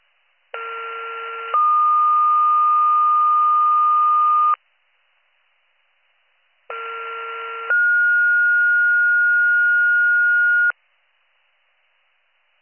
Two-tone page, heard as a one-second tone, then a three-second tone, about 1370 then 1220 Hz, about 1370 then 1430 Hz. It is heard from 0.5 to 10.7 s.